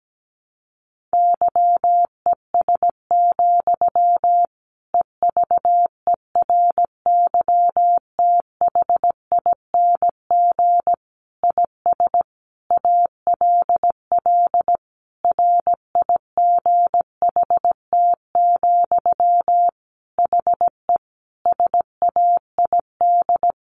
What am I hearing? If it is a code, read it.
Morse code, heard as a keyed tone, about 17 wpm, 700 Hz. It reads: YES, EVERYTHING IS ALL RIGHT, HE SAID
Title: Sentences from Top 1100 Words 17wpm